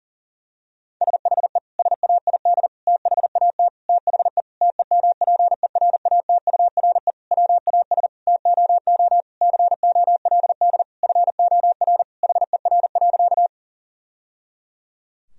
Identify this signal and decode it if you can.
Morse code, heard as a keyed tone, about 40 wpm, 700 Hz. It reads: SHE SAID THAT THE TEMPERATURE WAS TOO COLD FOR HER.
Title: Sentences from Top 1000 Words - Review Part 4 - 40wpm